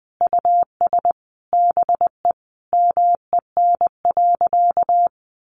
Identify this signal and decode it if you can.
Morse code, heard as a keyed tone, about 20 wpm, 700 Hz. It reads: US BE MEN.
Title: Sentences from Top 300 Words 20wpm